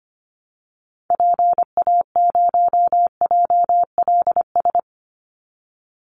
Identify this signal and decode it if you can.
Morse code, heard as a keyed tone, about 25 wpm, 700 Hz. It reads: PA0JLS
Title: Call Signs Repeated 2x - 25wpm